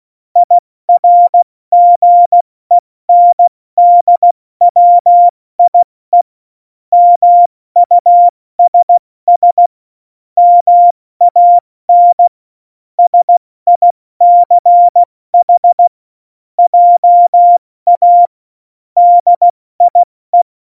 Morse code, heard as a keyed tone, about 16 wpm, 700 Hz. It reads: IRGENDWIE MUSS MAN SICH JA DIE
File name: Morsekurs / Witze / Witz 15. Januar 2019, 70 F80